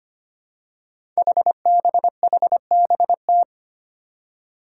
Morse code, heard as a keyed tone, about 25 wpm, 700 Hz. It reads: HBHBT
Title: Five Challenging Letter-Number Combinations 25wpm